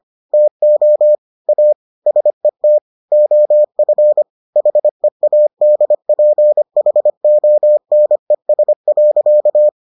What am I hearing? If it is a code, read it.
Morse code, heard as a keyed tone, about 25 wpm, 600 Hz. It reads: TO A SET OF HEADPHONES.